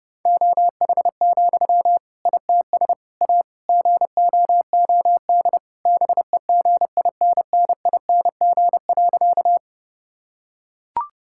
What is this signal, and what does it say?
Morse code, heard as a keyed tone, about 30 wpm, 700 Hz. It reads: OH, ITS A GOOD BEGINNING.
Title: Sentences from Top 1000 Words - Review Part 5 - 30wpm